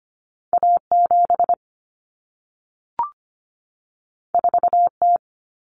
Morse code, heard as a keyed tone, about 25 wpm, 700 Hz. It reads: A7 4T
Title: Common Cut Numbers - 2-Digit 25wpm